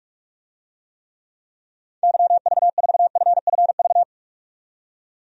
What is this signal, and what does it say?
Morse code, heard as a keyed tone, about 46 wpm, 700 Hz. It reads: YV4FFV
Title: DXCC-100 Call Signs - 6 Characters - 45wpm